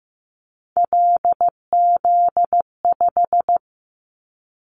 Morse code, heard as a keyed tone, about 15 wpm, 700 Hz. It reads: LZ5
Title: Callsign Prefixes for Top 50 DXCC - 15wpm